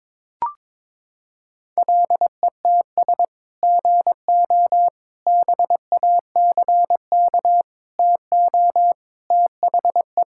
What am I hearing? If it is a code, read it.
Morse code, heard as a keyed tone, about 22 wpm, 700 Hz. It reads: LETS GO BACK TO THE
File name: practice / 22wpm / Sentences with 3-Word Phrases - 22wpm